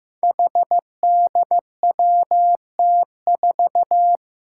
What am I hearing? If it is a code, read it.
Morse code, heard as a keyed tone, about 15 wpm, 700 Hz. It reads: HDWT4
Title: Five Challenging Letter-Number Combinations 15wpm